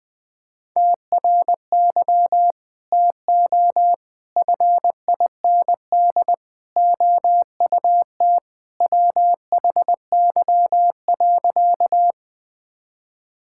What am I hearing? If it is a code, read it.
Morse code, heard as a keyed tone, about 20 wpm, 700 Hz. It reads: TRY TO FIND OUT WHY.
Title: Sentences with 3-Word Phrases - Spoken Beginning - 20wpm